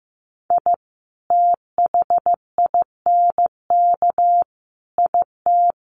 Morse code, heard as a keyed tone, about 14 wpm, 700 Hz. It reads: I THINK IT
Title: Sets of 5 Words Encore 15wpm